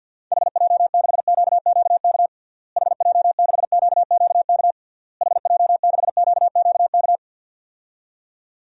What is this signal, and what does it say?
Morse code, heard as a keyed tone, about 50 wpm, 700 Hz. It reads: HJ6QQK HJ6QQK HJ6QQK
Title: DXCC-100 Call Signs - 6 Characters - Repeated 3x - 50wpm